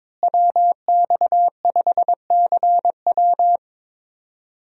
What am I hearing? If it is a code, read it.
Morse code, heard as a keyed tone, about 22 wpm, 700 Hz. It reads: WX5CW
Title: USA Call Signs - 5 Characters - Repeated 2x - 22wpm